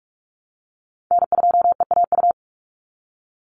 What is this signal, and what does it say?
Morse code, heard as a keyed tone, about 44 wpm, 700 Hz. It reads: N2EAU
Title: USA Call Signs 45wpm